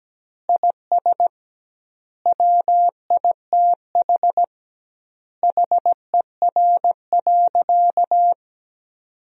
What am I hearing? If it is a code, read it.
Morse code, heard as a keyed tone, about 17 wpm, 700 Hz. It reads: IS WITH HER.
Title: 2x Word-Spacing - Sentences from Top 100 Words - 17wpm